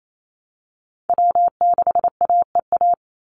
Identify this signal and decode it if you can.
Morse code, heard as a keyed tone, about 28 wpm, 700 Hz. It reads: W6AEA
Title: Call Signs - 28wpm